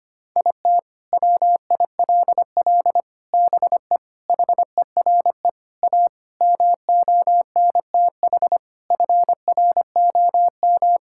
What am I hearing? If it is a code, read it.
Morse code, heard as a keyed tone, about 25 wpm, 700 Hz. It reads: IT WILL BE HERE A MONTH FROM
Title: Sentences from Top 1000 Words - Review Part 5 - 25wpm